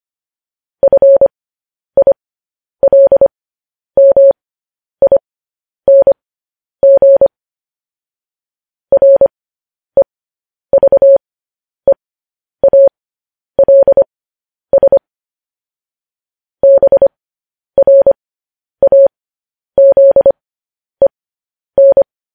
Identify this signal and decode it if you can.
Morse code, heard as a keyed tone, about 25 wpm, spaced as for 10 wpm, 550 Hz. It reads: FILMING REVEALS BRAZEN
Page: news-headlines-in-morse-code-at-10-wpm